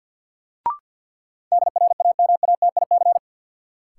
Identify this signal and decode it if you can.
Morse code, heard as a keyed tone, about 53 wpm, 700 Hz. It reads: DRAMATIC
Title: Sentences from Top 1300 Words - New Word Drill Alt - 50wpm